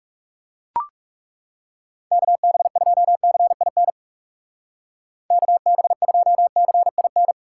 Morse code, heard as a keyed tone, about 38 wpm, 700 Hz. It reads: KB2CIN KB2CIN